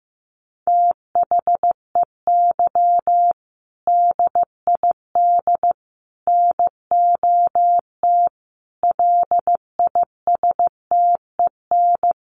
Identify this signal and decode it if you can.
Morse code, heard as a keyed tone, about 15 wpm, 700 Hz. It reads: THEY DID NOT LISTEN
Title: Sentences from Top 600 Words 15wpm